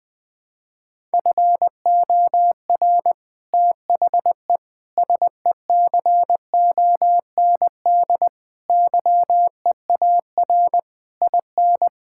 Morse code, heard as a keyed tone, about 20 wpm, 700 Hz. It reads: FOR THE SECOND YEAR IN
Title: Sets of 8 Words - 20wpm